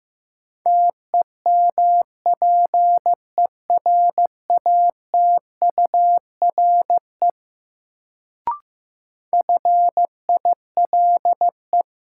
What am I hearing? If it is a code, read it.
Morse code, heard as a keyed tone, about 15 wpm, 700 Hz. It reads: TEMPERATURE FILE
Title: Sentences from Top 1000 Words - Review Part 4 - New Word Drill - 15wpm